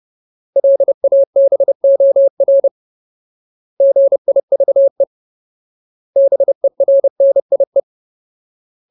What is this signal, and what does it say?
Morse code, heard as a keyed tone, about 30 wpm, 550 Hz. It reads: LABOR GIVE BERNIE
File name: Ultimate Word List / 30WPM Ultimate